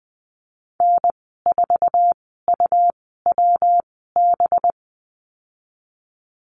Morse code, heard as a keyed tone, about 20 wpm, spaced as for 15 wpm, 700 Hz. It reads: N4UWB
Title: 193 Lesson 31g - Letter v - Call Signs 20wpm